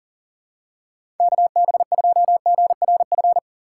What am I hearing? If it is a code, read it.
Morse code, heard as a keyed tone, about 40 wpm, 700 Hz. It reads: KB2GRF
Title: USA Call Signs - 6 Characters - 4x Character Spacing - 40wpm